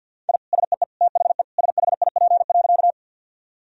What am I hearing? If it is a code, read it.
Morse code, heard as a keyed tone, about 51 wpm, 700 Hz. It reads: I SEE THE SHIP.
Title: Sentences from Top 1000 Words - Review Part 2 - 50wpm